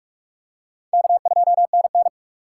Morse code, heard as a keyed tone, about 42 wpm, 700 Hz. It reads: K2NN
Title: USA Call Signs - 4 Characters - 4x Character Spacing - 45wpm